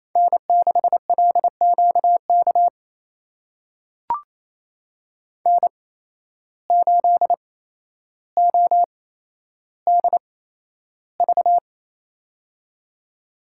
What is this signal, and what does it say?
Morse code, heard as a keyed tone, about 28 wpm, spaced as for 12 wpm, 700 Hz. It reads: N6LQK N8ODV
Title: USA Call Signs - 8x Character Spacing - 28wpm